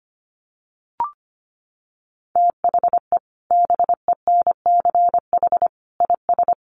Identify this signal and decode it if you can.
Morse code, heard as a keyed tone, about 25 wpm, 700 Hz. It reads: THE BENCH IS
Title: Sentences from Top 1400 Words 25wpm